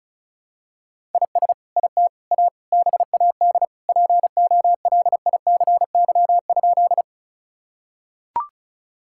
Morse code, heard as a keyed tone, about 36 wpm, 700 Hz. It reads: IS IT A BAD POLICY?
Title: Sentences from Top 1000 Words - Review Part 5 - 35wpm